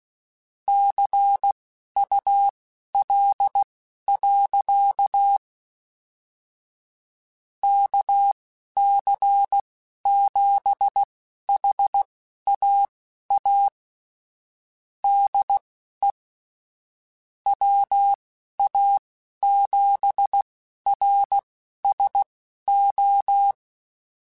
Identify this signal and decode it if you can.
Morse code, heard as a keyed tone, about 16 wpm, spaced as for 11 wpm, 800 Hz. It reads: CUL. KC7HAA DE WA7RSO
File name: QSOs / QSOs-mp3s / kc7haa@10